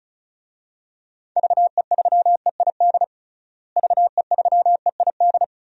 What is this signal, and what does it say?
Morse code, heard as a keyed tone, about 36 wpm, 700 Hz. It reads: VE3EID VE3EID
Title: DXCC-100 Call Signs - 6 Characters - Repeated 3x - 35wpm